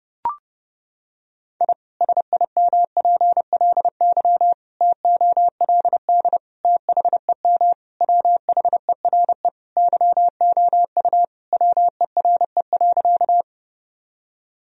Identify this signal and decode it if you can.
Morse code, heard as a keyed tone, about 28 wpm, 700 Hz. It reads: I SIMPLY TOLD THEM WHERE YOU WERE.
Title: Sentences from Top 1000 Words - Review Part 4 - 30wpm